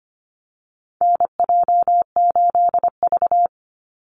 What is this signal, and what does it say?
Morse code, heard as a keyed tone, about 25 wpm, 700 Hz. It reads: NJ8V
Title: Call Signs - 25wpm